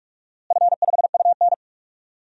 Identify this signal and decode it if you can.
Morse code, heard as a keyed tone, about 48 wpm, 700 Hz. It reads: F5UN
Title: DXCC-100 Call Signs - 4 Characters - Repeated 2x - 45wpm